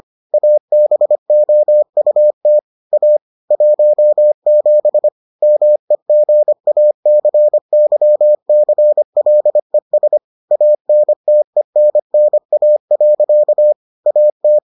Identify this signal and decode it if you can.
Morse code, heard as a keyed tone, about 25 wpm, 600 Hz. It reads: ABOUT A 17 MEGACYCLES ANTENNA. AT